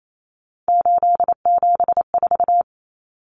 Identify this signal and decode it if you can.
Morse code, heard as a keyed tone, about 28 wpm, 700 Hz. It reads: 874
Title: Numbers - 3-Digit 28wpm